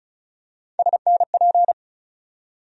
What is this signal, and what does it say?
Morse code, heard as a keyed tone, about 36 wpm, 700 Hz. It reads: SNP
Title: USA Call Sign Postfixes - 3 Letters 35wpm